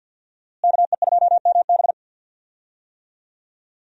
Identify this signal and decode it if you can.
Morse code, heard as a keyed tone, about 52 wpm, 700 Hz. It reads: KE2MB
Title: USA Call Signs - 5 Characters - Repeated 2x - 50wpm